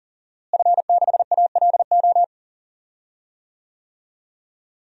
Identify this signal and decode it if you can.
Morse code, heard as a keyed tone, about 37 wpm, 700 Hz. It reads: F6ALO
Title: DXCC-100 Call Signs - 5 Characters - Repeated 3x - 40wpm